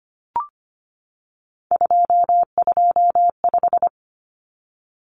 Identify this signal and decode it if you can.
Morse code, heard as a keyed tone, about 25 wpm, 700 Hz. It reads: 225
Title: Common Cut Numbers - 3-Digit 25wpm